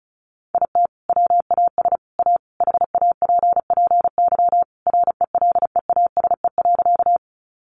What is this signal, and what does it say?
Morse code, heard as a keyed tone, about 34 wpm, 700 Hz. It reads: IT WAS A HAPPY RELEASE.
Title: Sentences from Top 1100 Words 35wpm